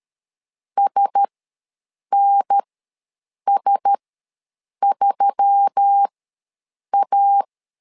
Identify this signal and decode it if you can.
Morse code, heard as a keyed tone, about 13 wpm, spaced as for 7 wpm, 800 Hz. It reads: SNS3A
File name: LEZIONI / 4 Q1S3 ws 12wpm 2038